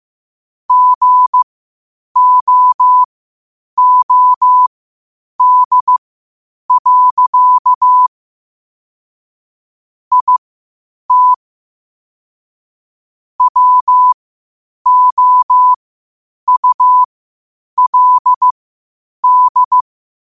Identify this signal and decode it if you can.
Morse code, heard as a keyed tone, about 15 wpm, spaced as for 8 wpm, 1000 Hz. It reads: GOOD. IT WOULD
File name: qotdfiles / 8wpm